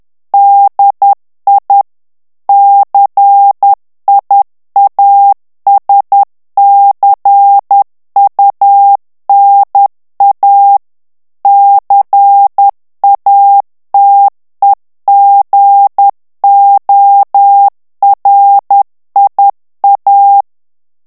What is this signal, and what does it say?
Morse code, heard as a keyed tone, about 10 wpm, 800 Hz. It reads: DI CIASCUNA CATEGORIA
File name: ITALIEN texte clair 2 vitesse 10 mots minute